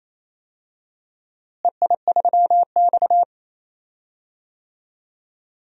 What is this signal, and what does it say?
Morse code, heard as a keyed tone, about 28 wpm, 700 Hz. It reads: EI3X
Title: DXCC-100 Call Signs - 4 Characters - Repeated 2x - 28wpm